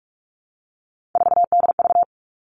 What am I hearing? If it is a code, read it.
Morse code, heard as a keyed tone, about 46 wpm, 700 Hz. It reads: 4DV
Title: Three Challenging Letter-Number Combinations 45wpm